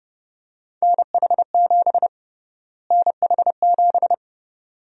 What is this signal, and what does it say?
Morse code, heard as a keyed tone, about 32 wpm, 700 Hz. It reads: NH7 NH7
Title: USA Call Sign Prefixes - Repeated 2x - 30wpm